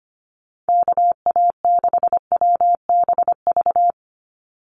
Morse code, heard as a keyed tone, about 25 wpm, 700 Hz. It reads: KA6WBV
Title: USA Call Signs 25wpm